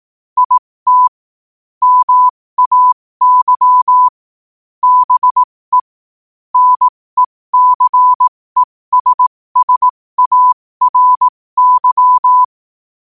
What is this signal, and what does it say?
Morse code, heard as a keyed tone, about 18 wpm, spaced as for 15 wpm, 1000 Hz. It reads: IT MAY BE NECESSARY
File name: qotdfiles / 15wpm